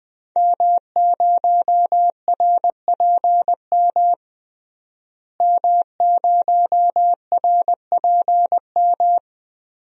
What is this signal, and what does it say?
Morse code, heard as a keyed tone, about 20 wpm, 700 Hz. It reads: M0RPM M0RPM
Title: DXCC-100 Call Signs - 5 Characters - Repeated 3x - 20wpm